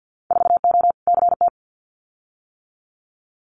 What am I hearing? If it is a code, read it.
Morse code, heard as a keyed tone, about 50 wpm, 700 Hz. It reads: 4O CT
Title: Winter Field Day Exchanges Encore 50wpm